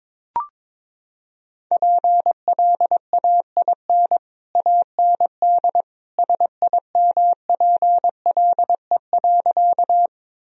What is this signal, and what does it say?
Morse code, heard as a keyed tone, about 22 wpm, 700 Hz. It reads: PLAIN AND SIMPLE.
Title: Sentences from Top 1000 Words - Review Part 2 - 22wpm